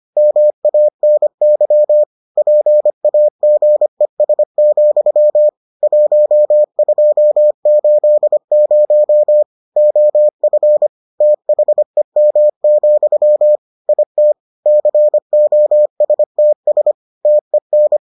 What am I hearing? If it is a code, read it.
Morse code, heard as a keyed tone, about 25 wpm, 600 Hz. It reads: MANY PAGES, 1280 OF THEM, IT COSTS TEN